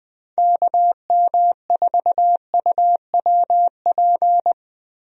Morse code, heard as a keyed tone, about 20 wpm, 700 Hz. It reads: KM4UWP
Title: USA Call Signs - 6 Characters - Repeated 3x - 20wpm